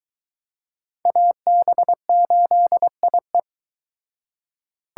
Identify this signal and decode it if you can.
Morse code, heard as a keyed tone, about 24 wpm, 700 Hz. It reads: AB8IE